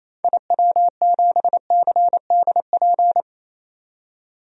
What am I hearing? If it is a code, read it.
Morse code, heard as a keyed tone, about 28 wpm, 700 Hz. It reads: IW7CDP
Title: DXCC-100 Call Signs - 6 Characters - 28wpm